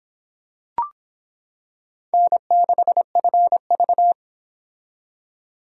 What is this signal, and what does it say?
Morse code, heard as a keyed tone, about 25 wpm, 700 Hz. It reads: N6FV